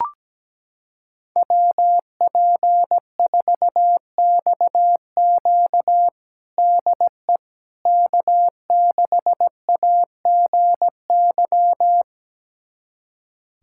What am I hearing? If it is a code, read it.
Morse code, heard as a keyed tone, about 17 wpm, 700 Hz. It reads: WP4XQ DE K6AGY
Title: USA Callsign DE Callsign 17wpm